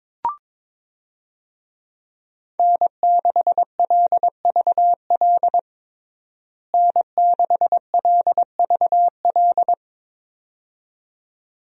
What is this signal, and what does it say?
Morse code, heard as a keyed tone, about 22 wpm, 700 Hz. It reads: N6LVL N6LVL
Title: USA Call Signs - 5 Characters - Repeated 2x - 22wpm